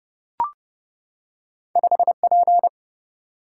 Morse code, heard as a keyed tone, about 30 wpm, 700 Hz. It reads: 5P
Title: Single-Letter Single-Number 30wpm